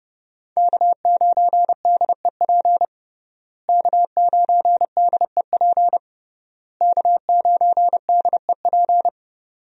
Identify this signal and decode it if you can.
Morse code, heard as a keyed tone, about 32 wpm, 700 Hz. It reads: K9DEP K9DEP K9DEP
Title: USA Call Signs - 5 Characters - Repeated 3x - 30wpm